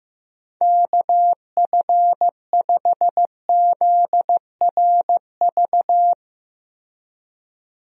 Morse code, heard as a keyed tone, about 15 wpm, 700 Hz. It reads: KF5ZRV
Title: USA Call Signs - 6 Characters - 15wpm